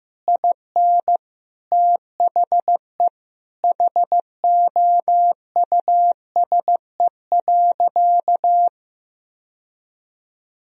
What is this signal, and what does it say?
Morse code, heard as a keyed tone, about 15 wpm, 700 Hz. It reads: IN THE HOUSE.
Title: Sentences from Top 1000 Words - Review Part 4 - 15wpm